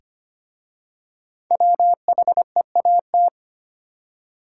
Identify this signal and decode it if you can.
Morse code, heard as a keyed tone, about 25 wpm, 700 Hz. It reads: WHEAT
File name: practice / 25wpm / Easy Words with Hints - 25wpm